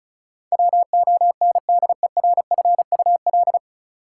Morse code, heard as a keyed tone, about 36 wpm, 700 Hz. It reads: WONDERFUL
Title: Sentences from Top 1100 Words - New Word Drill Alt - 35wpm